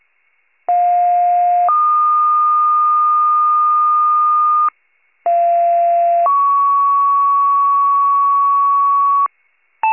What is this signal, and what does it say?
Two-tone page, heard as a one-second tone, then a three-second tone, about 710 then 1220 Hz, about 690 then 1120 Hz. It is heard from 0.7 to 9.3 s.
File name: wcalert